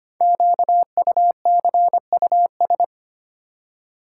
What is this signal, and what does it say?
Morse code, heard as a keyed tone, about 25 wpm, 700 Hz. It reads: QUCUS
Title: Code Groups - 5 Letters - 25wpm